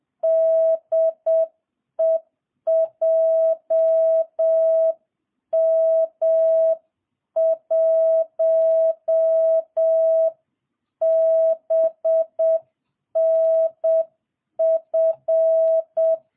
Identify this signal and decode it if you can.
Morse code, heard as a keyed tone, about 7 wpm, 660 Hz. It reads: DE JM1BNF